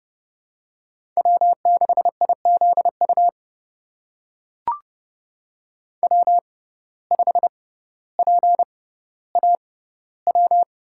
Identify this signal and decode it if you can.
Morse code, heard as a keyed tone, about 30 wpm, spaced as for 15 wpm, 700 Hz. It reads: W6IZU W5PAW
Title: USA Call Signs - 5 Characters - 6x Character Spacing - 30wpm